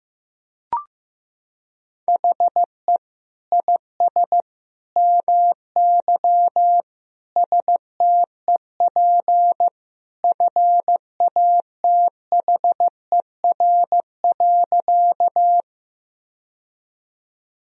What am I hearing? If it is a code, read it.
Morse code, heard as a keyed tone, about 16 wpm, 700 Hz. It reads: HE IS MY STEP FATHER.
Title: Sentences from Top 1000 Words - Review Part 2 - 15wpm